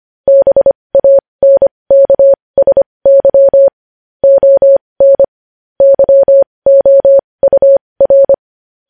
Morse code, heard as a keyed tone, about 25 wpm, spaced as for 20 wpm, 550 Hz. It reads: BANKSY ON YOUR
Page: news headlines in morse code at 20 wpm